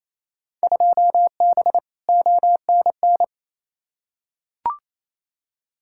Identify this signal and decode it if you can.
Morse code, heard as a keyed tone, about 28 wpm, 700 Hz. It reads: 2B ONN
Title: Field Day Exchanges Encore - 28wpm